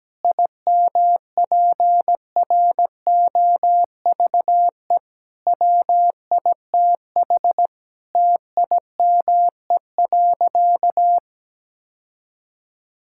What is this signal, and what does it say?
Morse code, heard as a keyed tone, about 17 wpm, 700 Hz. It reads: IMPROVE WITH TIME.
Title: Sentences from Top 1000 Words - Review Part 5 - 17wpm